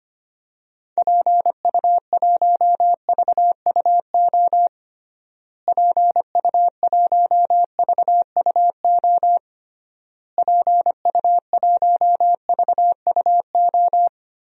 Morse code, heard as a keyed tone, about 25 wpm, 700 Hz. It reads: PU1VUO PU1VUO PU1VUO
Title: DXCC-100 Call Signs - 6 Characters - Repeated 3x - 25wpm